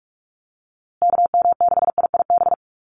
Morse code, heard as a keyed tone, about 44 wpm, 700 Hz. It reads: KM6IIB
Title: USA Call Signs Repeated 2x 45wpm